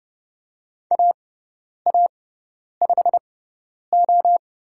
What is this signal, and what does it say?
Morse code, heard as a keyed tone, about 30 wpm, spaced as for 10 wpm, 700 Hz. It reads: AA5O
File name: Lesson 18g - Letter m - Call Signs 30wpm